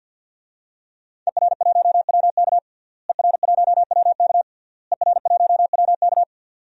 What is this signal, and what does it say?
Morse code, heard as a keyed tone, about 52 wpm, 700 Hz. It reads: ER1WK ER1WK ER1WK
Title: DXCC-100 Call Signs - 5 Characters - Repeated 3x - 50wpm